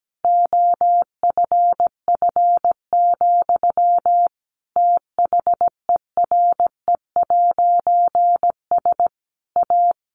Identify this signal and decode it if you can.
Morse code, heard as a keyed tone, about 17 wpm, 700 Hz. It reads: OFF, THERE'S A
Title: A Sound Of Thunder 17wpm